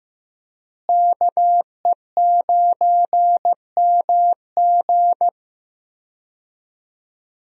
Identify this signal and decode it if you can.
Morse code, heard as a keyed tone, about 15 wpm, 700 Hz. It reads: KE9MG
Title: USA Call Signs - 5 Characters - Repeated 3x - 15wpm